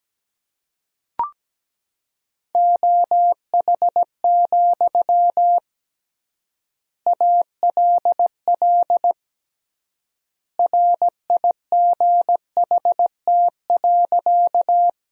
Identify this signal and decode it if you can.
Morse code, heard as a keyed tone, about 17 wpm, 700 Hz. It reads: OH, ALL RIGHT.
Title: 3x Word-Spacing - Sentences from Top 500 Words - 17wpm